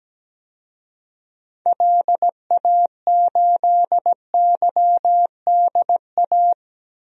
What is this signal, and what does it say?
Morse code, heard as a keyed tone, about 17 wpm, 700 Hz. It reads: LA8YDA
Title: DXCC-100 Call Signs - 6 Characters - 17wpm